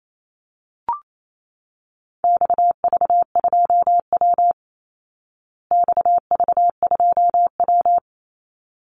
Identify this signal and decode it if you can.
Morse code, heard as a keyed tone, about 28 wpm, 700 Hz. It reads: XV2W XV2W
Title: Call Signs Repeated 3x - 28wpm